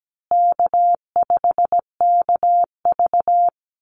Morse code, heard as a keyed tone, about 17 wpm, 700 Hz. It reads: K5KV
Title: Easy Call Signs Repeated 2x - 17wpm